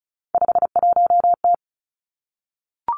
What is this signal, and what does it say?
Morse code, heard as a keyed tone, about 36 wpm, 700 Hz. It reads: 51T
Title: Three Challenging Letter-Number Combinations 35wpm